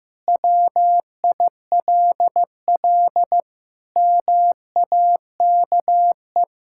Morse code, heard as a keyed tone, about 15 wpm, 700 Hz. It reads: WILL MAKE
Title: Sentences from Top 1000 Words - Review Part 6 - 15wpm